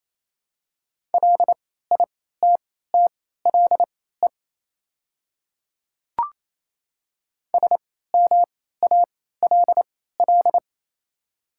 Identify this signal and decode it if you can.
Morse code, heard as a keyed tone, about 28 wpm, spaced as for 16 wpm, 700 Hz. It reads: LITTLE SMALL
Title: 3x Character Spacing - Top 200 Words - 28wpm